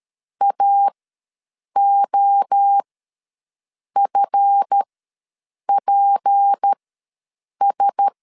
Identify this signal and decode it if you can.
Morse code, heard as a keyed tone, about 13 wpm, spaced as for 7 wpm, 800 Hz. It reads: AOFPS
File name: LEZIONI / 7 GF9Z / 7 GF9Z ws 12wpm 3321